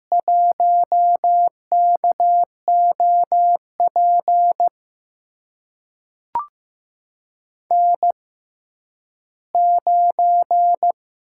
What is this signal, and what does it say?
Morse code, heard as a keyed tone, about 15 wpm, spaced as for 10 wpm, 700 Hz. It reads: 1KOP N9
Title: USA Call Signs - 6x Character Spacing - 15wpm